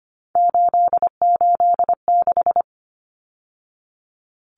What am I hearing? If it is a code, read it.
Morse code, heard as a keyed tone, about 25 wpm, 700 Hz. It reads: 886
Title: Numbers - 3-Digit 25wpm